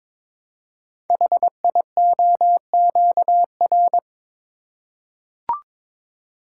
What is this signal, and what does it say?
Morse code, heard as a keyed tone, about 22 wpm, 700 Hz. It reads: HIOQR
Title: Code Groups - 5 Letter:Numbers - 22wpm